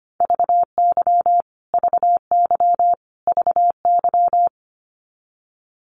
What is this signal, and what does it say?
Morse code, heard as a keyed tone, about 25 wpm, 700 Hz. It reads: VY VY VY
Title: Abbreviations Repeated 3x - 25wpm